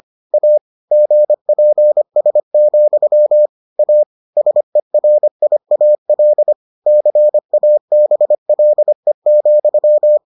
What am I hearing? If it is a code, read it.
Morse code, heard as a keyed tone, about 25 wpm, 600 Hz. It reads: A GPS, A SERIAL CABLE,